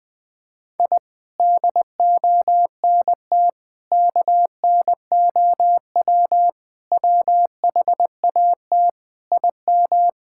Morse code, heard as a keyed tone, about 20 wpm, 700 Hz. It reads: I DONT KNOW WHAT IM
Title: Sets of 8 Words - 20wpm